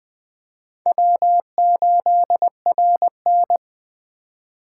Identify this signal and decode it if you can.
Morse code, heard as a keyed tone, about 20 wpm, 700 Hz. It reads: W8RN
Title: USA Call Signs - 4 Characters - 6x Character Spacing - 20wpm